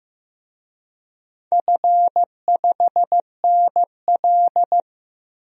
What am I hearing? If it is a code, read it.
Morse code, heard as a keyed tone, about 15 wpm, 700 Hz. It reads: F5NL
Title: DXCC-100 Call Signs - 4 Characters - Repeated 2x - 15wpm